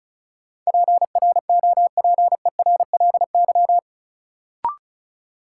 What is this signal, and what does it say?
Morse code, heard as a keyed tone, about 34 wpm, 700 Hz. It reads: PROPERLY
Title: Sentences from Top 1100 Words - New Word Drill Alt - 35wpm